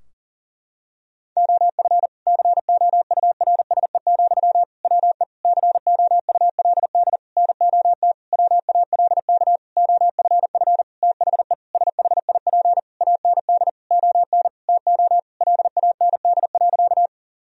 Morse code, heard as a keyed tone, about 40 wpm, 700 Hz. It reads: OF COURSE, WE COULD NOT WALK OFF THE SHIP AND ON TO LAND.
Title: Sentences from Top 1000 Words - Review Part 2 - 40wpm